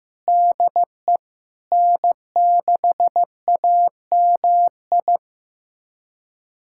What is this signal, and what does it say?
Morse code, heard as a keyed tone, about 15 wpm, 700 Hz. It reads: DE N6AMI
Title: USA Callsign DE Callsign 15wpm